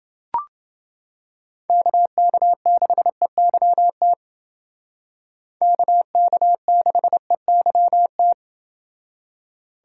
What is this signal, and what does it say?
Morse code, heard as a keyed tone, about 29 wpm, 700 Hz. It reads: KK6EYT KK6EYT